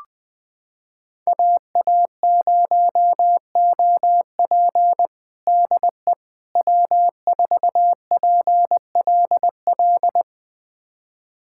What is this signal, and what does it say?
Morse code, heard as a keyed tone, about 20 wpm, 700 Hz. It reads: AA0OP DE W4PLL
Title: USA Callsign DE Callsign - 5 Characters - 20wpm